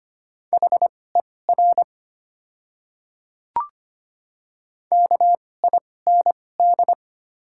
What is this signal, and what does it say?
Morse code, heard as a keyed tone, about 25 wpm, spaced as for 18 wpm, 700 Hz. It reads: HER KIND
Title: 2x Character Spacing - Top 200 Words - 25wpm